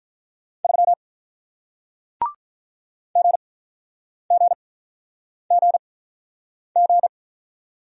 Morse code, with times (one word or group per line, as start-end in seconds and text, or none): 0.6-0.9 3
3.2-7.1 GGGG